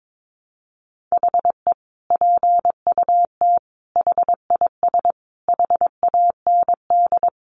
Morse code, heard as a keyed tone, about 22 wpm, 700 Hz. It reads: HE PUT HIS HAND
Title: Sentences from Top 400 Words 22wpm